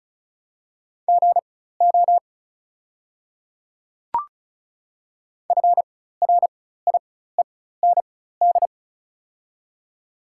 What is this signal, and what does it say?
Morse code, heard as a keyed tone, about 35 wpm, spaced as for 16 wpm, 700 Hz. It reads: GO FRIEND
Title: 4x Character Spacing - Top 300 Words - 35wpm